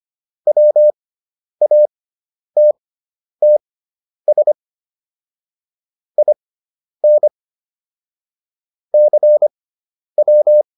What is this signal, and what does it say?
Morse code, heard as a keyed tone, about 25 wpm, spaced as for 10 wpm, 600 Hz. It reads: WATTS IN CW